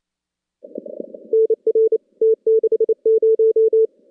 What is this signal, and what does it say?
Morse code, heard as a keyed tone, about 28 wpm, 410 Hz. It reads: NR T60
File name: contests / CW-SS-k5tr- -2200